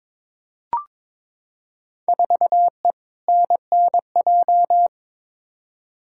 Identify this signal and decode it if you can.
Morse code, heard as a keyed tone, about 22 wpm, 700 Hz. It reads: 4E NNJ
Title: Field Day Exchanges - 22wpm